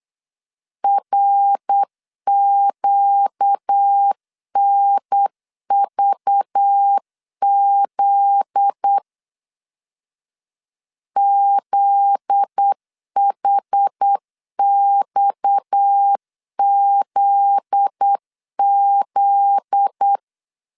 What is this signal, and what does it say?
Morse code, heard as a keyed tone, about 8 wpm, 800 Hz. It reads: RQNVZ ZHXZZ
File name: LEZIONI / 7 GF9Z / 7 GF9Z 8wpm 3213